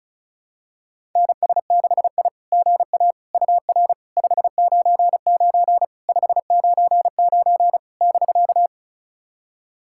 Morse code, heard as a keyed tone, about 36 wpm, 690 Hz. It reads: NS6I GA UR 599 599 <BK>
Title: SOTA QSO Practice 35wpm